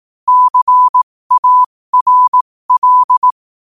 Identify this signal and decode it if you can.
Morse code, heard as a keyed tone, about 18 wpm, spaced as for 15 wpm, 1000 Hz. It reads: CARL
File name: qotdfiles / 15wpm